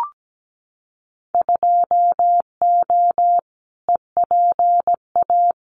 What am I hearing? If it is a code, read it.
Morse code, heard as a keyed tone, about 17 wpm, 700 Hz. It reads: 2O EPA
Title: Winter Field Day Exchanges Encore 17wpm